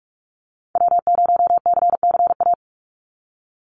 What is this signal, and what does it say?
Morse code, heard as a keyed tone, about 46 wpm, 700 Hz. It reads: W0CCA
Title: NA SOTA Activator Callsigns 45wpm